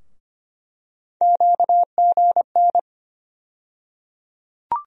Morse code, heard as a keyed tone, about 25 wpm, 700 Hz. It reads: QGN
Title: USA Call Sign Postfixes - 3 Letters 25wpm